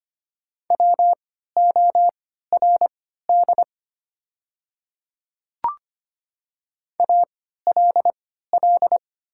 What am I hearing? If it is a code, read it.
Morse code, heard as a keyed tone, about 25 wpm, spaced as for 14 wpm, 700 Hz. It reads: WORD ALL
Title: 3x Character Spacing - Top 100 Words - 25wpm